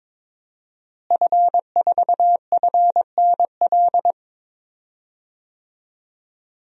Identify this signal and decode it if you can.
Morse code, heard as a keyed tone, about 22 wpm, 700 Hz. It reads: F4FNL
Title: DXCC-100 Call Signs - 5 Characters - Repeated 3x - 22wpm